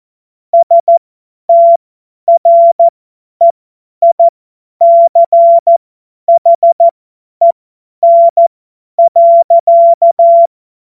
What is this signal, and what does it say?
Morse code, heard as a keyed tone, about 14 wpm, spaced as for 10 wpm, 680 Hz. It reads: STREICHEN.
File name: Morsekurs / Witze / Witz 24.Oktober 2023 50 F70